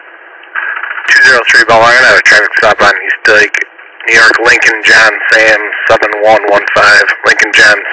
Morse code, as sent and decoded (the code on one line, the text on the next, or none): none